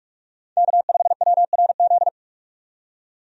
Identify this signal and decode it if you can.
Morse code, heard as a keyed tone, about 46 wpm, 700 Hz. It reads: K5WRZ
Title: USA Call Signs - 5 Characters - 45wpm